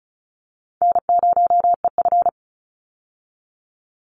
Morse code, heard as a keyed tone, about 34 wpm, 700 Hz. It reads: N0EF
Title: Call Signs - 35wpm